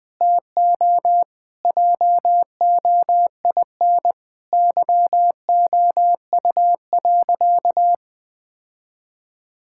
Morse code, heard as a keyed tone, about 20 wpm, 700 Hz. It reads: TO JOIN YOU.
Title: Sentences with 5-Word Phrases - 20wpm